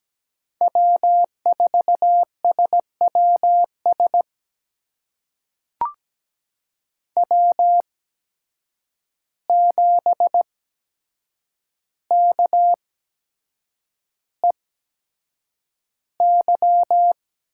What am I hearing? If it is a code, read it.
Morse code, heard as a keyed tone, about 17 wpm, spaced as for 7 wpm, 700 Hz. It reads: W4SWS W7KEY
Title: USA Call Signs - 5 Characters - 8x Character Spacing - 17wpm